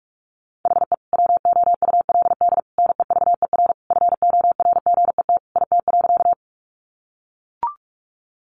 Morse code, heard as a keyed tone, about 44 wpm, 700 Hz. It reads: HE WOULD NEVER FORGET IT.
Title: Sentences from Top 1200 Words 45wpm